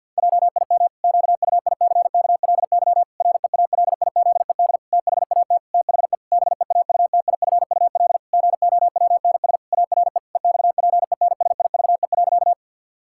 Morse code, heard as a keyed tone, about 48 wpm, 700 Hz. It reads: JIM QUICKLY REALIZED THAT THE BEAUTIFUL GOWNS ARE EXPENSIVE.
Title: Pangram Challenge v2 - 50wpm